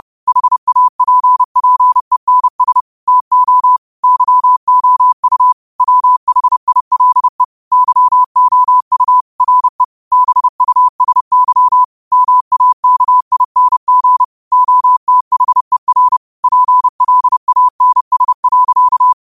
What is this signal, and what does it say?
Morse code, heard as a keyed tone, about 30 wpm, 1000 Hz. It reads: HAPPENS TO YOU WHILE YOU RE BUSY MAKING OTHER PLANS.
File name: qotdfiles / 30wpm